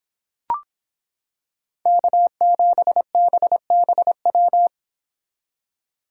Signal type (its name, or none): Morse code